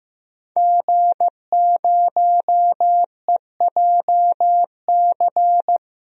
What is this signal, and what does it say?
Morse code, heard as a keyed tone, about 15 wpm, 700 Hz. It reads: G0EJC
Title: DXCC-100 Call Signs - 5 Characters - Repeated 3x - 15wpm